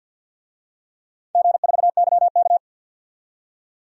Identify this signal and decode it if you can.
Morse code, heard as a keyed tone, about 52 wpm, 700 Hz. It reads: G4YK
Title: DXCC-100 Call Signs - 4 Characters - 50wpm